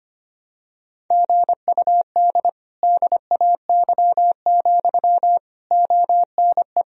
Morse code, heard as a keyed tone, about 25 wpm, 700 Hz. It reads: GUD DAY, ONE
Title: Sentences from Top 200 Words and Abbreviations 25wpm